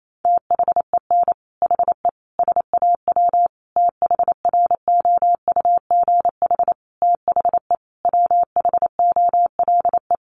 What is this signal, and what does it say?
Morse code, heard as a keyed tone, about 28 wpm, 700 Hz. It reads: THEN HE SAW THROUGH THE WHOLE
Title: Sentences from Top 400 Words 28wpm